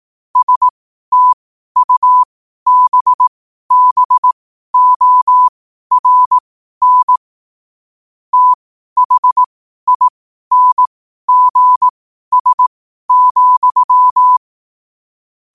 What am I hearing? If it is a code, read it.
Morse code, heard as a keyed tone, about 18 wpm, spaced as for 12 wpm, 1000 Hz. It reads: STUBBORN THINGS,
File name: qotdfiles / 12wpm